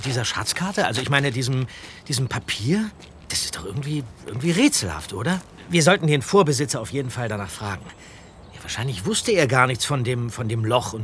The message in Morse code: none